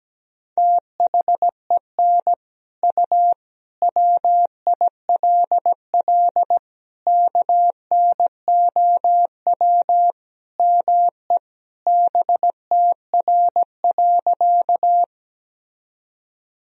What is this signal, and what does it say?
Morse code, heard as a keyed tone, about 17 wpm, 700 Hz. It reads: THEN U WILL KNOW ME BTR.
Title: Sentences from Top 400 Words and Abbreviations 17wpm